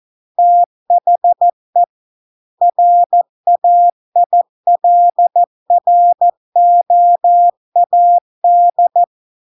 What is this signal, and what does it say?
Morse code, heard as a keyed tone, about 14 wpm, 700 Hz. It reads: THE RAILROAD